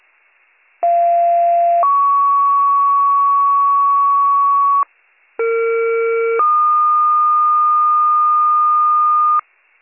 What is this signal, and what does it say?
Two-tone page, heard as a one-second tone, then a three-second tone, about 690 then 1120 Hz, about 460 then 1220 Hz. It is heard from 0.8 to 9.4 s.